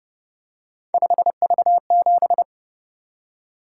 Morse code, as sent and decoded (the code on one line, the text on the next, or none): ..... ...- --...
5V7